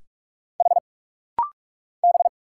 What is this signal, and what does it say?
Morse code, heard as a keyed tone, about 46 wpm, 700 Hz. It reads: H B
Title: Single Letter-Number Alt - Rapid-Fire 45wpm